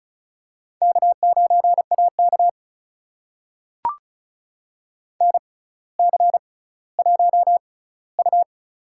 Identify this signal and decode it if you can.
Morse code, heard as a keyed tone, about 36 wpm, spaced as for 18 wpm, 700 Hz. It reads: K9AK NC1U
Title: USA Call Signs - 4 Characters - 6x Character Spacing - 35wpm